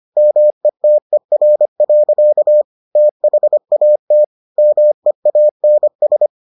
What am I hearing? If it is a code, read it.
Morse code, heard as a keyed tone, about 25 wpm, 600 Hz. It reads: METER. THAT MEANS